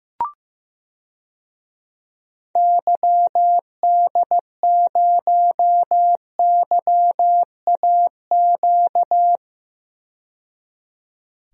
Morse code, heard as a keyed tone, about 15 wpm, 700 Hz. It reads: YD0YAQ
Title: DXCC-100 Call Signs - 6 Characters - 15wpm